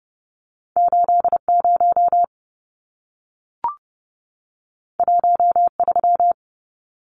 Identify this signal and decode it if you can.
Morse code, heard as a keyed tone, about 30 wpm, 700 Hz. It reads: 80 13
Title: Numbers - 2-Digit 30wpm